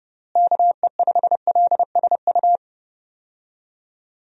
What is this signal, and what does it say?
Morse code, heard as a keyed tone, about 30 wpm, 700 Hz. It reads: KE5LSU
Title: USA Call Signs - 6 Characters - Repeated 2x - 30wpm